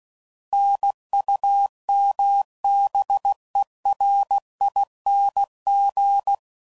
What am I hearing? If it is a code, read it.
Morse code, heard as a keyed tone, about 16 wpm, 800 Hz. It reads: NUMBERING
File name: Suffixes / Special INGs / Sngl Ings